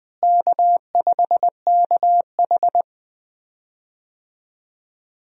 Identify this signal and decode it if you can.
Morse code, heard as a keyed tone, about 20 wpm, 700 Hz. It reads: K5KH
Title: USA Call Signs - 4 Characters - Repeated 2x - 20wpm